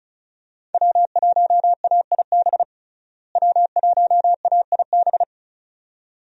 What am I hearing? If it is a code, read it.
Morse code, heard as a keyed tone, about 36 wpm, 700 Hz. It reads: W1AIB W1AIB
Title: USA Call Signs - 5 Characters - Repeated 2x - 35wpm